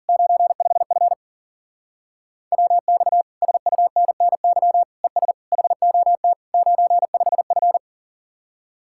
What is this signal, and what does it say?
Morse code, heard as a keyed tone, about 50 wpm, 700 Hz. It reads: 95F WX SUNNY ES HOT 95F